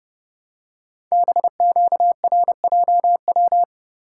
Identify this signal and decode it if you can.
Morse code, heard as a keyed tone, about 30 wpm, 700 Hz. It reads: BQRJW
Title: Code Groups - 5 Letters - 30wpm